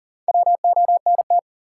Morse code, heard as a keyed tone, about 40 wpm, 700 Hz. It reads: WONT
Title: Sentences from Top 1000 Words - Review Part 6 - New Word Drill - 40wpm